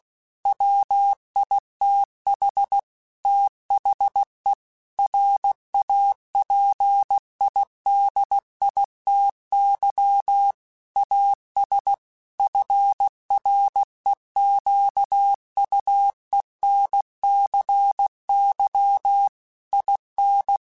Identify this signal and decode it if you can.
Morse code, heard as a keyed tone, about 16 wpm, 800 Hz. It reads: WITH THE RAPIDITY AS FREQUENCY IN
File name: Technical Subjects / Technical Subjects-mp3s / XC @15